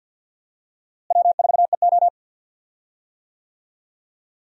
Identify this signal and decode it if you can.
Morse code, heard as a keyed tone, about 53 wpm, 700 Hz. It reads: W4EO
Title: USA Call Signs - 4 Characters - Repeated 3x - 50wpm